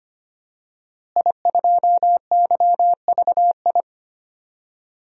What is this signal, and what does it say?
Morse code, heard as a keyed tone, about 25 wpm, 700 Hz. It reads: I2YVI
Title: DXCC-100 Call Signs - 5 Characters - Repeated 3x - 25wpm